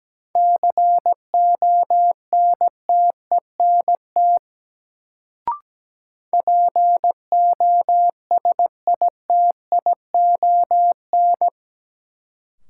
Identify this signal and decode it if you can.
Morse code, heard as a keyed tone, about 17 wpm, 700 Hz. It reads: CONTENT POSITION
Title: Sentences from Top 900 Words - New Word Drill - 17wpm